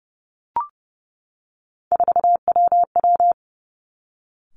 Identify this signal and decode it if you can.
Morse code, heard as a keyed tone, about 30 wpm, 700 Hz. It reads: 4WW
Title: Three Challenging Letter-Number Combinations 30wpm